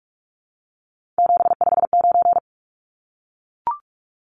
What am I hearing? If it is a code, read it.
Morse code, heard as a keyed tone, about 44 wpm, 700 Hz. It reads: 759
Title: Numbers - 3-Digit 45wpm